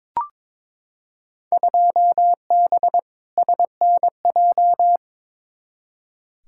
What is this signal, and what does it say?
Morse code, heard as a keyed tone, about 22 wpm, 700 Hz. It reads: 2B SNJ
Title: Field Day Exchanges Encore - 22wpm